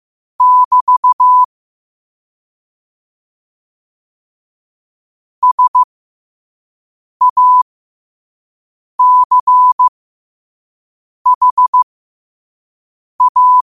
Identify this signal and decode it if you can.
Morse code, heard as a keyed tone, about 15 wpm, spaced as for 5 wpm, 1000 Hz. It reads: <BT> SACHA
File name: qotdfiles / 5wpm